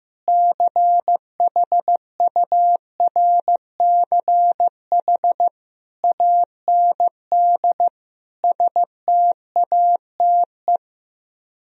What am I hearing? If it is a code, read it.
Morse code, heard as a keyed tone, about 15 wpm, 700 Hz. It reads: CHURCH AND STATE
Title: Speed-Racing - Sets of 5 Words 15wpm